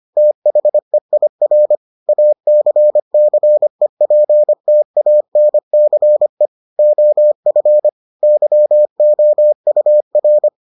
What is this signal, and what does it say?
Morse code, heard as a keyed tone, about 25 wpm, 600 Hz. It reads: THEIR ACCEPTANCE OF YOUR